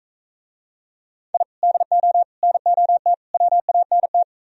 Morse code, heard as a keyed tone, about 43 wpm, 700 Hz. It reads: I DO NOT WANT